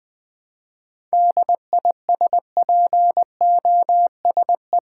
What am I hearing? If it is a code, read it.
Morse code, heard as a keyed tone, about 20 wpm, 700 Hz. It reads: DISPOSE
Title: 7 Letter Words - First 3 Letters Given - 20wpm